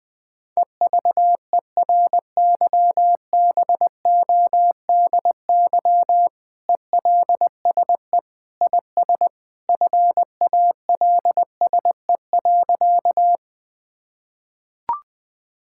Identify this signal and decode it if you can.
Morse code, heard as a keyed tone, about 20 wpm, 700 Hz. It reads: EVERYBODY ELSE IS FALSE.
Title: Sentences from Top 1600 Words 20wpm